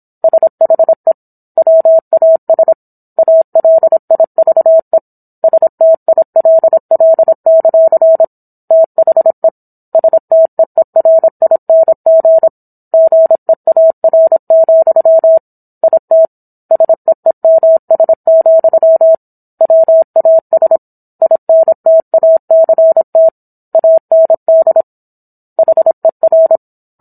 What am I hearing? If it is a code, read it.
Morse code, heard as a keyed tone, about 27 wpm, 650 Hz. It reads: SHE WAS ALIVE STILL; THE STEERING GEAR, IT SEEMS, WAS INTACT AND HER